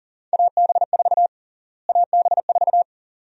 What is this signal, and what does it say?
Morse code, heard as a keyed tone, about 40 wpm, 700 Hz. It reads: AB4 AB4
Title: USA Call Sign Prefixes - Repeated 2x - 40wpm